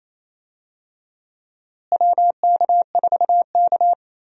Morse code, heard as a keyed tone, about 28 wpm, 700 Hz. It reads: WK4K
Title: USA Call Signs - 4 Characters - Repeated 3x - 28wpm